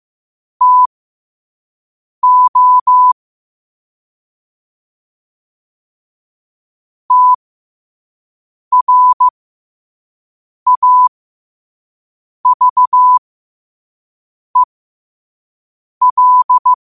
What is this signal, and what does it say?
Morse code, heard as a keyed tone, about 15 wpm, spaced as for 5 wpm, 1000 Hz. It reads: TO TRAVEL
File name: qotdfiles / 5wpm